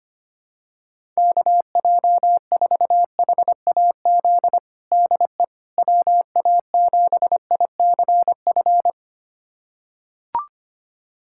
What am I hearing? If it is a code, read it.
Morse code, heard as a keyed tone, about 25 wpm, 700 Hz. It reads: KJ4HAZ DE WA7ICF
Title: USA Callsign DE Callsign - 6 Characters - 25wpm